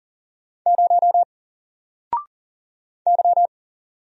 Morse code, heard as a keyed tone, about 41 wpm, 700 Hz. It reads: 0 Y
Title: Single Letter-Number Alt 40wpm